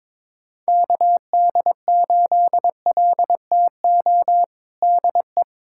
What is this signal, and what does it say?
Morse code, heard as a keyed tone, about 22 wpm, 700 Hz. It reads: KD8LTO DE
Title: USA Callsign DE Callsign - 6 Characters - 22wpm